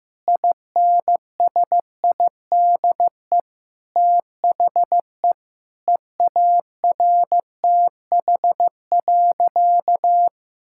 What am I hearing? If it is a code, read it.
Morse code, heard as a keyed tone, about 15 wpm, 700 Hz. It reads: INSIDE THE EARTH.
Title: Sentences from Top 1000 Words - Review Part 6 - 15wpm